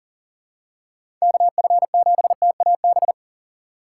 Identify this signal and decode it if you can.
Morse code, heard as a keyed tone, about 38 wpm, 700 Hz. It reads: KF7TAB
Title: USA Call Signs - 6 Characters - 2x Character Spacing - 40wpm